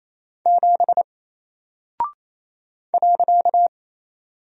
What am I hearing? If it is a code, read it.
Morse code, heard as a keyed tone, about 28 wpm, 700 Hz. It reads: 7 .
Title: Single Character (Letter:Number:Punctuation:Prosign) 28wpm